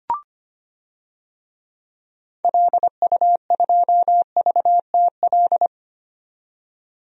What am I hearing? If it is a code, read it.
Morse code, heard as a keyed tone, about 25 wpm, 700 Hz. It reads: LU2VTL
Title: DXCC-100 Call Signs - 6 Characters - 25wpm